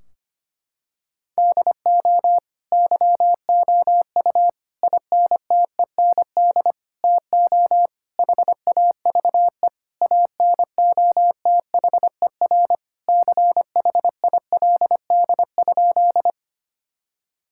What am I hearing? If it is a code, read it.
Morse code, heard as a keyed tone, about 25 wpm, 700 Hz. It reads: DO YOU INTEND TO HAVE ANOTHER CHILD?
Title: Sentences from Top 1600 Words 25wpm